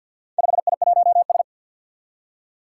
Morse code, heard as a keyed tone, about 46 wpm, 700 Hz. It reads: 5I1S
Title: Four Challenging Letter-Number Combinations 50wpm